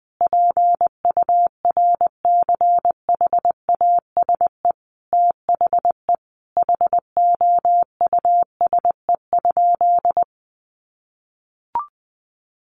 Morse code, heard as a keyed tone, about 20 wpm, 700 Hz. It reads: PURCHASE THE HOUSE?
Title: Sentences from Top 800 Words 20wpm